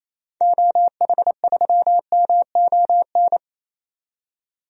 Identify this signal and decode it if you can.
Morse code, heard as a keyed tone, about 28 wpm, 700 Hz. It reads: OH3MON
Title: DXCC-100 Call Signs - 6 Characters - 28wpm